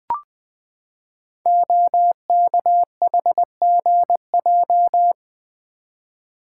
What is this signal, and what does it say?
Morse code, heard as a keyed tone, about 20 wpm, 700 Hz. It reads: OKHGJ
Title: Code Groups - 5 Letter:Numbers - 20wpm